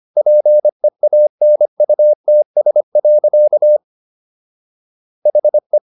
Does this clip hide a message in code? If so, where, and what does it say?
Yes: Morse code from 0.2 to 5.8 s, heard as a keyed tone, about 25 wpm, 600 Hz. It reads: PEANUTS. HE